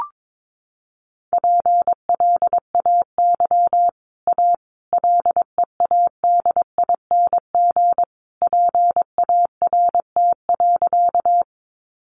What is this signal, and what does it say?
Morse code, heard as a keyed tone, about 22 wpm, 700 Hz. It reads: PLAY A LEADING PART.
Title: Sentences from Top 1100 Words 22wpm